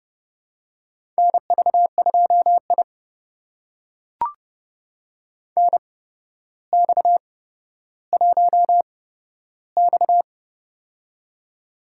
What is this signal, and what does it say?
Morse code, heard as a keyed tone, about 30 wpm, spaced as for 13 wpm, 700 Hz. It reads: NV2I NX1X
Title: USA Call Signs - 4 Characters - 8x Character Spacing - 30wpm